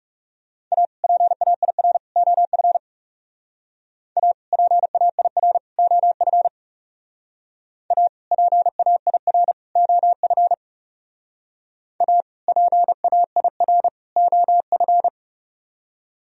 Morse code, heard as a keyed tone, about 37 wpm, 700 Hz. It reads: A PAIR OF A PAIR OF A PAIR OF A PAIR OF
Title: Speed-Racing - Sets of 3 Words 30wpm